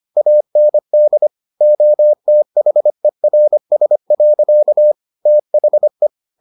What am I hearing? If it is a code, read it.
Morse code, heard as a keyed tone, about 25 wpm, 600 Hz. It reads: AND OTHERS. THE